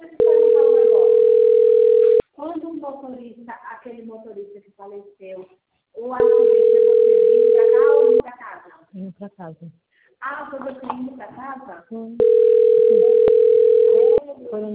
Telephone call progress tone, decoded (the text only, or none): ringback tone